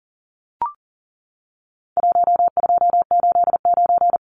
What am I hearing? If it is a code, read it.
Morse code, heard as a keyed tone, about 40 wpm, 700 Hz. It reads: 1289
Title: Numbers - Contest Style - 4-Digit 40wpm